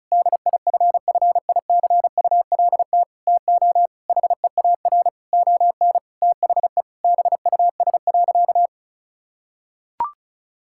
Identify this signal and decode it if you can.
Morse code, heard as a keyed tone, about 34 wpm, 700 Hz. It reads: DIFFICULT TO HEAR ON THE BUS.
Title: Sentences from Top 1000 Words - Review Part 6 - 35wpm